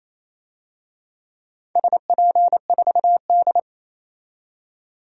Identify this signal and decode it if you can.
Morse code, heard as a keyed tone, about 28 wpm, 690 Hz. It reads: SP4D
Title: DXCC-100 Call Signs - 4 Characters - 28wpm